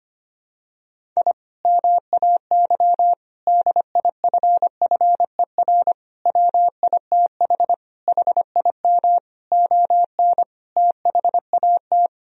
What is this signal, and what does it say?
Morse code, heard as a keyed tone, about 25 wpm, 700 Hz. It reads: I MAY DIFFER WITH HIM ON THAT
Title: Sentences from Top 1000 Words - Review Part 1 - 25wpm